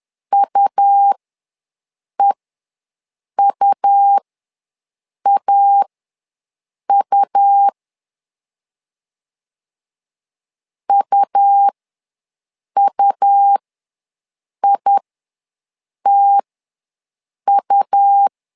Morse code, heard as a keyed tone, about 11 wpm, spaced as for 6 wpm, 800 Hz. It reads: UEUAU UUITU
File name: LEZIONI / 2 UHDN / 2 U ws 10wpm 2017